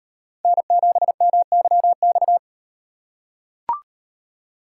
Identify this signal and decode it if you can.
Morse code, heard as a keyed tone, about 38 wpm, 700 Hz. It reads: N7MYX